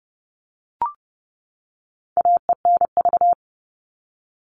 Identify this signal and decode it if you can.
Morse code, heard as a keyed tone, about 30 wpm, 700 Hz. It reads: AENV
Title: All Cut Numbers - 4-Digit 30wpm